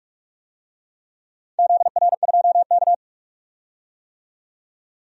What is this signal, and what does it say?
Morse code, heard as a keyed tone, about 44 wpm, 700 Hz. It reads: ZR2K
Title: DXCC-100 Call Signs - 4 Characters - 45wpm